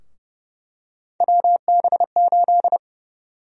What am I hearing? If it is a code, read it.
Morse code, heard as a keyed tone, about 30 wpm, 700 Hz. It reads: WB8
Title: USA Call Sign Prefixes - 30wpm